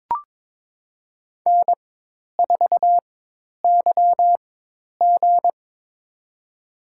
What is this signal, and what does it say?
Morse code, heard as a keyed tone, about 22 wpm, spaced as for 10 wpm, 700 Hz. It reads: N4YG
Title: USA Call Signs - 4x Character Spacing - 22wpm